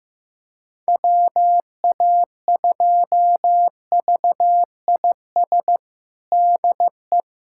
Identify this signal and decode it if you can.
Morse code, heard as a keyed tone, about 15 wpm, 700 Hz. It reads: WA2VIS DE
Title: USA Callsign DE Callsign - 6 Characters - 15wpm